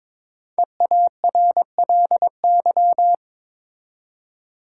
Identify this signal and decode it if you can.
Morse code, heard as a keyed tone, about 22 wpm, 700 Hz. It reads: EARLY
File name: 5 Letter Words - First 3 Letters Given - 22wpm